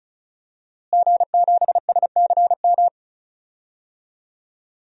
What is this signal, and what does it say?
Morse code, heard as a keyed tone, about 34 wpm, 700 Hz. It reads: G7SCM
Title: DXCC-100 Call Signs - 5 Characters - Repeated 2x - 35wpm